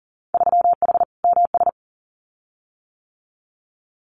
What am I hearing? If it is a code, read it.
Morse code, heard as a keyed tone, about 42 wpm, 700 Hz. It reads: 3H MS
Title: Winter Field Day Exchanges Encore 40wpm